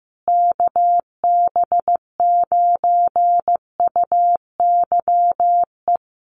Morse code, heard as a keyed tone, about 15 wpm, 700 Hz. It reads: KB9UYE
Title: USA Call Signs Repeated 3x 15wpm